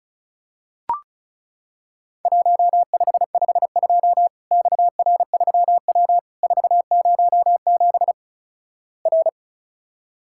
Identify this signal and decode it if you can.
Morse code, heard as a keyed tone, about 36 wpm, 700 Hz. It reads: 1552 XR3W 407 R
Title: Worked All Europe DX Contest - QTC Exchange - 35wpm